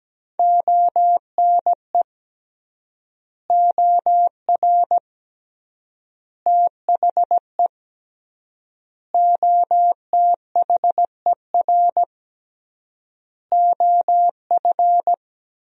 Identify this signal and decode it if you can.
Morse code, heard as a keyed tone, about 17 wpm, 700 Hz. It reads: ONE OR THE OTHER OF
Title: 3x Word-Spacing - Sentences from Top 400 Words - 17wpm